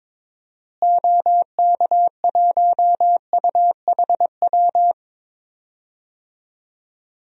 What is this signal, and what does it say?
Morse code, heard as a keyed tone, about 22 wpm, 700 Hz. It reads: OK1UHW
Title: DXCC-100 Call Signs - 6 Characters - Repeated 3x - 22wpm